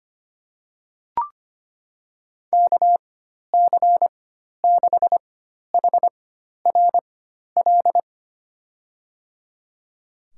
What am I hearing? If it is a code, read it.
Morse code, heard as a keyed tone, about 25 wpm, spaced as for 12 wpm, 700 Hz. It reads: KC6HRL
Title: USA Call Signs - 4x Character Spacing - 25wpm